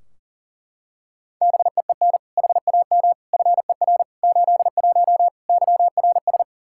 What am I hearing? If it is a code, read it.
Morse code, heard as a keyed tone, about 40 wpm, 700 Hz. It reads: BEEN HAM FER 81 YRS
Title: Practice QSO - Exchange 3 - More Info - 40wpm